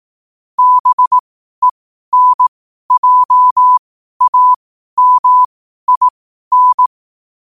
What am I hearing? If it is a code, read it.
Morse code, heard as a keyed tone, about 18 wpm, spaced as for 12 wpm, 1000 Hz. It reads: BENJAMIN
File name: qotdfiles / 12wpm